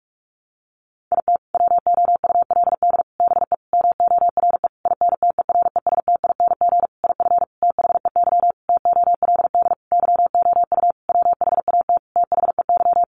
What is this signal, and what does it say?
Morse code, heard as a keyed tone, about 46 wpm, 700 Hz. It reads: IT WOULD BE MORE INTERESTING IF THEY TOLD YOU WHAT THEY
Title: Sentences from Top 1000 Words 45wpm